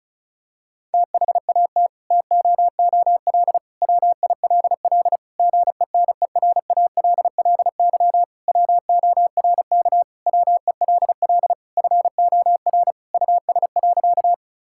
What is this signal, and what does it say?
Morse code, heard as a keyed tone, about 36 wpm, 700 Hz. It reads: THAT TOOL WILL GENERALLY WORK WELL FOR US.
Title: Sentences from Top 1000 Words - Review Part 6 - 35wpm